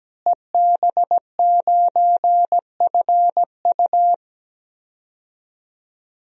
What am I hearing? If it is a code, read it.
Morse code, heard as a keyed tone, about 17 wpm, 700 Hz. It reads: EB9FU
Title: DXCC-100 Call Signs - 5 Characters - Repeated 3x - 17wpm